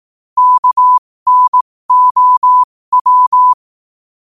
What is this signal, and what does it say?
Morse code, heard as a keyed tone, about 18 wpm, spaced as for 15 wpm, 1000 Hz. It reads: KNOW
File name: qotdfiles / 15wpm